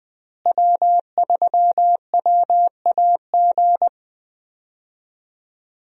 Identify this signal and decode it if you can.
Morse code, heard as a keyed tone, about 20 wpm, 700 Hz. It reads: W3WAG
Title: USA Call Signs - 5 Characters - Repeated 2x - 20wpm